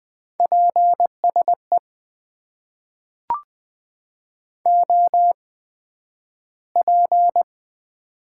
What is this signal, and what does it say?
Morse code, heard as a keyed tone, about 20 wpm, spaced as for 11 wpm, 700 Hz. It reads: PSE OP
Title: Top 90 QSO Elements - 8x Character Spacing - 20wpm